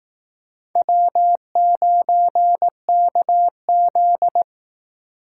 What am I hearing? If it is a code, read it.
Morse code, heard as a keyed tone, about 19 wpm, 700 Hz. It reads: W9KZ